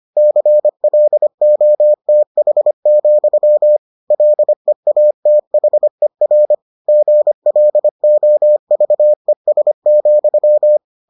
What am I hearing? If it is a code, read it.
Morse code, heard as a keyed tone, about 25 wpm, 600 Hz. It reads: CLOTH, LEATHER GLOVES,